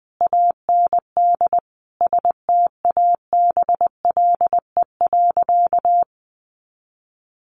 Morse code, heard as a keyed tone, about 20 wpm, 700 Hz. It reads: AND STABLE.
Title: Sentences from Top 1500 Words 20wpm